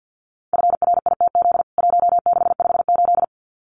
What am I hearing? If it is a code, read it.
Morse code, heard as a keyed tone, about 48 wpm, 700 Hz. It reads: FRITZ 1658
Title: CWT Exchange 50wpm